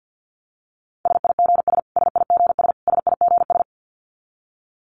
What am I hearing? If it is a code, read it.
Morse code, heard as a keyed tone, about 50 wpm, 700 Hz. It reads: SIGS SIGS SIGS
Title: Abbreviations Repeated 3x - 50wpm